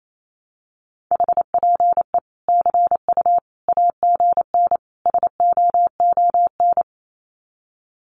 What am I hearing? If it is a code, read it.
Morse code, heard as a keyed tone, about 28 wpm, 700 Hz. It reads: HPE CU AGN SOON
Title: Common QSO Phrases 28wpm